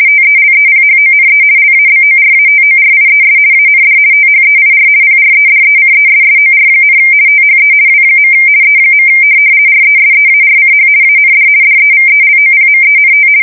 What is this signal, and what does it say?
RTTY, about 50 baud. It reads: TELEPHONE LABORATORIES AND THEIR PREDECESSOR, THE W. E. RESEARCH AND DEVELOPMENT
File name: fsk170-50B / teletype-corp